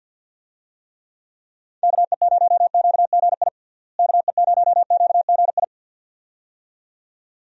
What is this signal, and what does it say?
Morse code, heard as a keyed tone, about 51 wpm, 700 Hz. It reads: KE0QGI KE0QGI
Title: USA Call Signs - 6 Characters - Repeated 2x - 50wpm